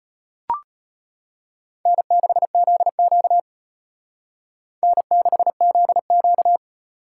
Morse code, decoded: N6ZQ N6ZQ

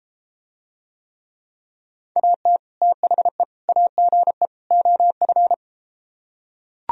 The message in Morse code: AT THE AGE OF

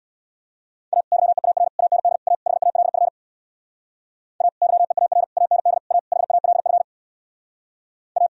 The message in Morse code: A YEAR AND A HALF A YEAR AND A HALF A

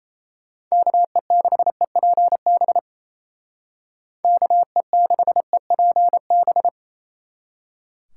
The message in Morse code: KE6EPB KE6EPB

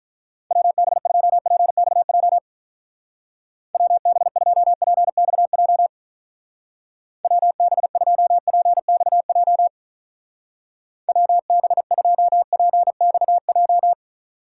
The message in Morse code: WB2PXJ WB2PXJ WB2PXJ WB2PXJ

